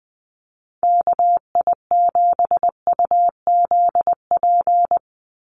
Morse code, heard as a keyed tone, about 20 wpm, 700 Hz. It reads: KI7UZP
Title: USA Call Signs 20wpm